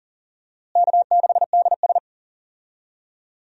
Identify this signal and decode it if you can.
Morse code, heard as a keyed tone, about 40 wpm, 700 Hz. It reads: K6DS
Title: USA Call Signs - 4 Characters - Repeated 2x - 40wpm